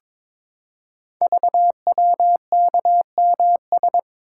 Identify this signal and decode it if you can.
Morse code, heard as a keyed tone, about 22 wpm, 700 Hz. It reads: VWKMS
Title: Code Groups - 5 Letters - 22wpm